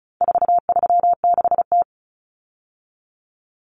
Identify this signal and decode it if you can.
Morse code, heard as a keyed tone, about 34 wpm, 700 Hz. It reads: 436T
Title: Common Cut Numbers - 4-Digit 35wpm